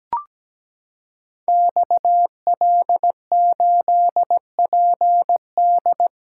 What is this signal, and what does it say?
Morse code, heard as a keyed tone, about 17 wpm, 700 Hz. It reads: XL8PD
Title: Code Groups - 5 Letter:Numbers - 17wpm